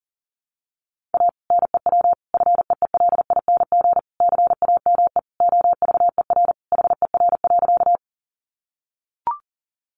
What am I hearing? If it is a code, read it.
Morse code, heard as a keyed tone, about 40 wpm, 700 Hz. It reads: A NEW FEELING CAME OVER HER.
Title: Sentences from Top 1200 Words 40wpm